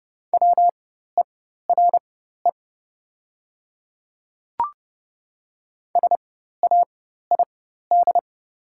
Morse code, heard as a keyed tone, about 30 wpm, spaced as for 14 wpm, 700 Hz. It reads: WERE SAID
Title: 4x Character Spacing - Top 100 Words - 30wpm